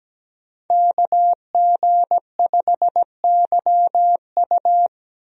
Morse code, heard as a keyed tone, about 17 wpm, 700 Hz. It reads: KG5YU
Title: USA Call Signs - 5 Characters - Repeated 2x - 17wpm